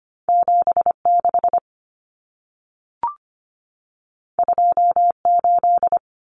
Morse code, heard as a keyed tone, about 25 wpm, 700 Hz. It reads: 76 28
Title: Numbers - 2-Digit 25wpm